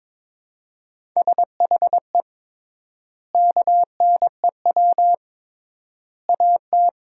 Morse code, heard as a keyed tone, about 22 wpm, 700 Hz. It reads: SHE KNEW AT
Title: 3x Word-Spacing - Sentences from Top 400 Words - 22wpm